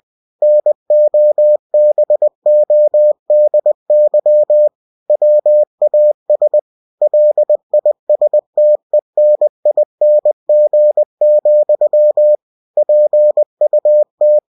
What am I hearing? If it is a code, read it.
Morse code, heard as a keyed tone, about 20 wpm, 600 Hz. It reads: NOBODY WAS LISTENING, PUT
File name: cw-books / winnie-the-pooh / winnie-20wpm-page-0005